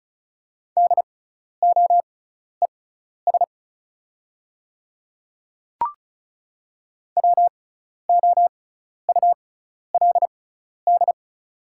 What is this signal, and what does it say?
Morse code, heard as a keyed tone, about 36 wpm, spaced as for 12 wpm, 700 Hz. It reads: DOES WOULD
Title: 6x Character Spacing - Top 300 Words - 35wpm